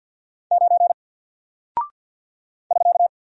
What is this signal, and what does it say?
Morse code, heard as a keyed tone, about 48 wpm, 700 Hz. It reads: 9 <SK>
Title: Single Character (Letter:Number:Punctuation:Prosign) 50wpm